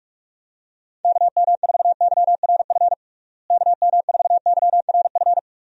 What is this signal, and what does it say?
Morse code, heard as a keyed tone, about 46 wpm, 700 Hz. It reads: KM4YRF KM4YRF
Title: USA Call Signs - 6 Characters - Repeated 3x - 45wpm